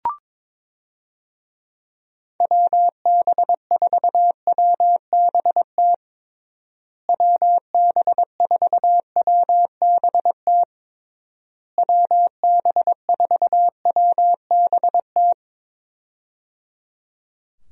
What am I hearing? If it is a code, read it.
Morse code, heard as a keyed tone, about 22 wpm, 700 Hz. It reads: WB4WBT WB4WBT WB4WBT
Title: USA Call Signs - 6 Characters - Repeated 3x - 22wpm